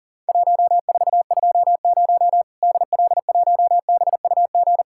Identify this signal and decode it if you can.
Morse code, heard as a keyed tone, about 40 wpm, 700 Hz. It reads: 1420 DL1BUG
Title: Worked All Europe DX Contest - QTC Exchange - 40wpm